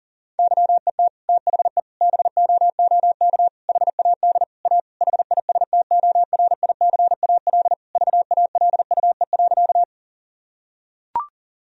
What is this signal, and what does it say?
Morse code, heard as a keyed tone, about 40 wpm, 700 Hz. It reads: YET THE BOOK HAD A HISTORICAL VALUE.
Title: Sentences from Top 1000 Words - Review Part 4 - 40wpm